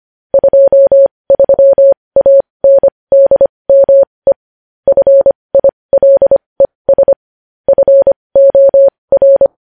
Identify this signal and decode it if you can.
Morse code, heard as a keyed tone, about 25 wpm, spaced as for 20 wpm, 550 Hz. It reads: 23ANDME FILES FOR
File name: podcast / News headlines-1x-25x20